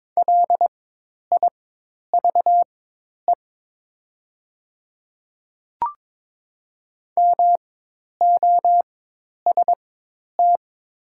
Morse code, heard as a keyed tone, about 22 wpm, spaced as for 10 wpm, 700 Hz. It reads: LIVE MOST
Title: 4x Character Spacing - Top 200 Words - 22wpm